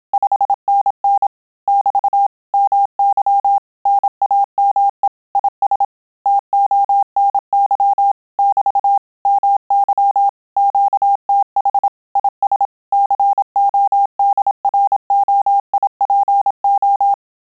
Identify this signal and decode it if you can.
Morse code, heard as a keyed tone, about 27 wpm, 800 Hz. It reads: 5NN <BT> MY NAME IS TONY <BT> MY QTH IS CODROIPO
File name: Telegraphy-group / QSO CW 25 PPM